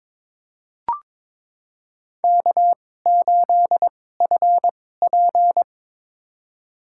Morse code, heard as a keyed tone, about 22 wpm, spaced as for 16 wpm, 700 Hz. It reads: K8FP
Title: USA Call Signs - 4 Characters - 2x Character Spacing - 22wpm